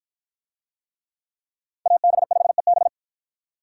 Morse code, heard as a keyed tone, about 51 wpm, 700 Hz. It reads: AB5EB